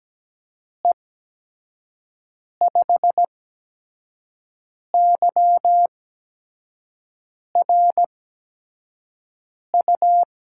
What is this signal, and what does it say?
Morse code, heard as a keyed tone, about 17 wpm, spaced as for 5 wpm, 700 Hz. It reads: E5YRU
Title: USA Call Signs - 6 Characters - 8x Character Spacing - 17wpm